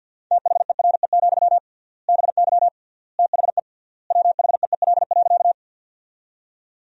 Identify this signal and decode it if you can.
Morse code, heard as a keyed tone, about 52 wpm, 700 Hz. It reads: THERE, BY THE WHEEL.
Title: 3x Word-Spacing - Sentences from Top 500 Words - 50wpm